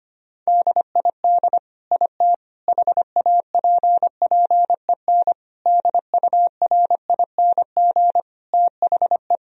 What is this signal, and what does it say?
Morse code, heard as a keyed tone, about 25 wpm, 700 Hz. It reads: DID IT HAPPEN DURING THE
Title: Sentences from Top 1000 Words - Review Part 2 - 25wpm